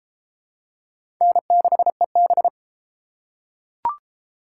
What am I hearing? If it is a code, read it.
Morse code, heard as a keyed tone, about 29 wpm, 700 Hz. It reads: N6EB